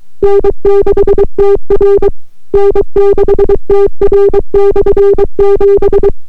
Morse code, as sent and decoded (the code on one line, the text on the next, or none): -. -.... - .-. / -. -.... - .-. -..-. --...
N6TR N6TR/7